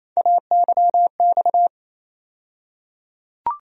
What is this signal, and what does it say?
Morse code, heard as a keyed tone, about 28 wpm, 700 Hz. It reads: AYX
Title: USA Call Sign Postfixes - 3 Letters 28wpm